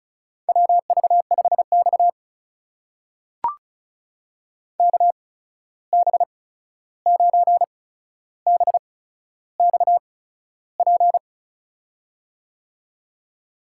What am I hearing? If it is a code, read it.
Morse code, heard as a keyed tone, about 36 wpm, spaced as for 13 wpm, 700 Hz. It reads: WV5X KB9BXP
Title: USA Call Signs - 8x Character Spacing - 35wpm